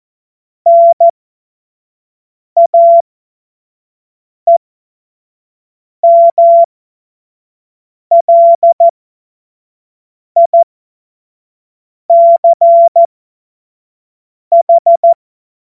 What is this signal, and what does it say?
Morse code, heard as a keyed tone, about 14 wpm, spaced as for 5 wpm, 680 Hz. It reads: NAEMLICH
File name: Morsekurs / Witze / Witz 18. April 2023 25 F70